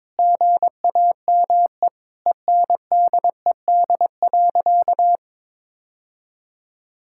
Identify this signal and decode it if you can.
Morse code, heard as a keyed tone, about 22 wpm, 700 Hz. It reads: GAME ENDED.
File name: practice / 22wpm / Sentences with 2-Word Phrases - 22wpm